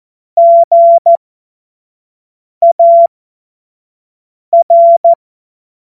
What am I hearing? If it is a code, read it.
Morse code, heard as a keyed tone, about 14 wpm, spaced as for 5 wpm, 680 Hz. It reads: GAR
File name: Morsekurs / Witze / Witz 01. Juni 2021, 25 F70